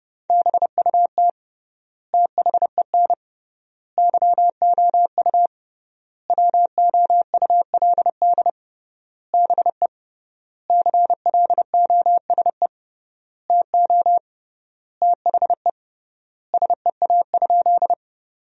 Morse code, heard as a keyed tone, about 30 wpm, 700 Hz. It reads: BUT THEN YOU WOULD BE CLOSE TO THE SEA?
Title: 3x Word-Spacing - Sentences from Top 300 Words - 30wpm